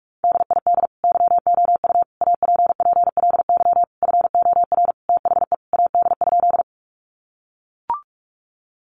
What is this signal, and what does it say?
Morse code, heard as a keyed tone, about 44 wpm, 700 Hz. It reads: DID YOU APPLY FOR THE AD?
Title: Sentences from Top 1300 Words 45wpm